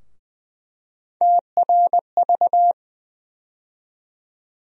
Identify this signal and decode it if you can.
Morse code, heard as a keyed tone, about 20 wpm, 700 Hz. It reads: TRV
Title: USA Call Sign Postfixes - 3 Letters 20wpm